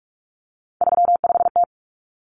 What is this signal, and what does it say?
Morse code, heard as a keyed tone, about 44 wpm, 700 Hz. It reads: 35T
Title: Common Cut Numbers - 3-Digit 45wpm